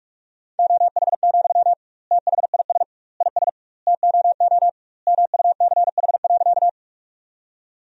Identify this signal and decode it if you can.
Morse code, heard as a keyed tone, about 46 wpm, 700 Hz. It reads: OH, THIS IS TOO MUCH.
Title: 2x Word-Spacing - Sentences from Top 500 Words - 45wpm